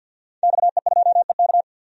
Morse code, heard as a keyed tone, about 46 wpm, 700 Hz. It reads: XE2EK
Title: DXCC-100 Call Signs - 5 Characters - Repeated 3x - 50wpm